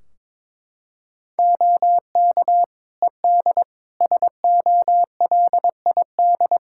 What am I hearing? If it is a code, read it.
Morse code, heard as a keyed tone, about 21 wpm, 700 Hz. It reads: OK ED SOLID
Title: K1USN SST QSO Practice 22wpm